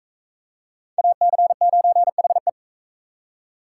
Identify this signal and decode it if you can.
Morse code, heard as a keyed tone, about 42 wpm, 700 Hz. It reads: AC9HE